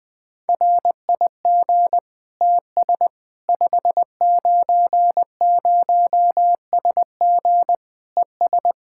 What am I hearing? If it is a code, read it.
Morse code, heard as a keyed tone, about 20 wpm, 700 Hz. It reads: RIG TS 590SG ES
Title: Practice QSO - Exchange 2 - Info - 20wpm